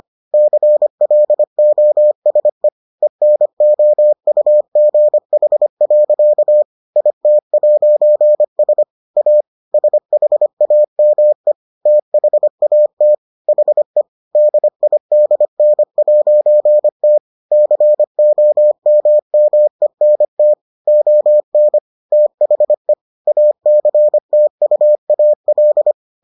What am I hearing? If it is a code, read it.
Morse code, heard as a keyed tone, about 25 wpm, 600 Hz. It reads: CLOSE ENOUGH. IT'S A SHAME THAT HE DIDN'T COMMENT ON THE ACTUAL